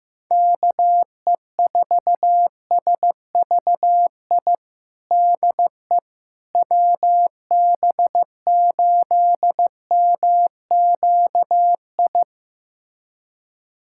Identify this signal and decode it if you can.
Morse code, heard as a keyed tone, about 16 wpm, 700 Hz. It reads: KE4SVI DE WB8MQI
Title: USA Callsign DE Callsign - 6 Characters - 15wpm